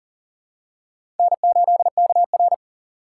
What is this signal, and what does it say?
Morse code, heard as a keyed tone, about 40 wpm, 700 Hz. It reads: N8KR
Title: USA Call Signs - 4 Characters - 3x Character Spacing - 40wpm